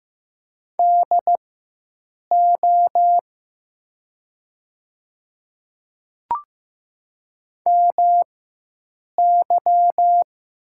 Morse code, heard as a keyed tone, about 15 wpm, spaced as for 7 wpm, 700 Hz. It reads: DO MY
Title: 4x Character Spacing - Top 100 Words - 15wpm